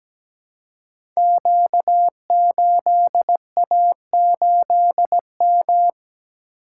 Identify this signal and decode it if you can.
Morse code, heard as a keyed tone, about 17 wpm, 700 Hz. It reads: Q8A8M
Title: Code Groups - 5 Letter:Numbers - 17wpm